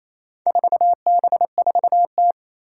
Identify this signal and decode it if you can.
Morse code, heard as a keyed tone, about 27 wpm, 700 Hz. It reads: 4B4T
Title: Four Challenging Letter-Number Combinations 28wpm